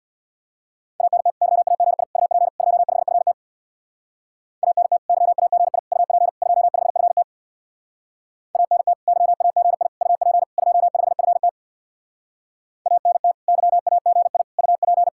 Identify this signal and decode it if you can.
Morse code, heard as a keyed tone, about 66 wpm, 700 Hz. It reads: ANT YAGI UP 25FT ANT YAGI UP 25FT ANT YAGI UP 25FT ANT YAGI UP